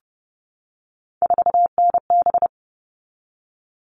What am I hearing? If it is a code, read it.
Morse code, heard as a keyed tone, about 30 wpm, 700 Hz. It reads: 4NB
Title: All Cut Numbers - 3-Digit 30wpm